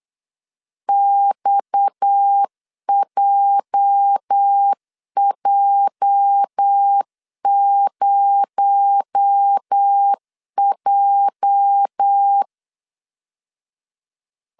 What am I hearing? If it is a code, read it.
Morse code, heard as a keyed tone, about 8 wpm, 800 Hz. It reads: XJJ0J
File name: LEZIONI / 8 YJ45 / 8 YJ 8wpm 3313